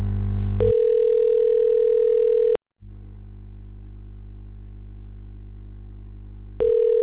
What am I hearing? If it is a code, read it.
Ringback tone (North America), 440 + 480 Hz.